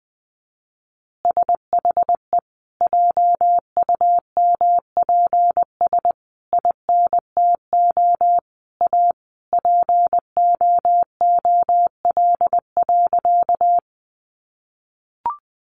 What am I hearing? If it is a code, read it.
Morse code, heard as a keyed tone, about 20 wpm, 700 Hz. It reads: SHE JUMPS INTO A POOL.
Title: Easy Phrases 20wpm